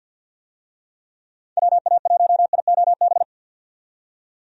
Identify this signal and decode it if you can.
Morse code, heard as a keyed tone, about 46 wpm, 700 Hz. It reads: WA1IOB
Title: USA Call Signs - 6 Characters - 50wpm